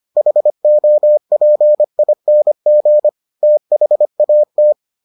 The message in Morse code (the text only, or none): HOPING THAT